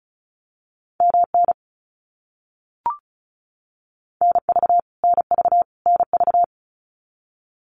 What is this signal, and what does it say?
Morse code, heard as a keyed tone, about 34 wpm, 700 Hz. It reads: MN NV NV NV
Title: US States Repeated 3x 35wpm